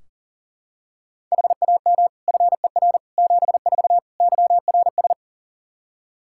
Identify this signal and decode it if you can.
Morse code, heard as a keyed tone, about 40 wpm, 700 Hz. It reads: HAM FER 74 YRS
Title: Practice QSO - Exchange 3 - More Info - 40wpm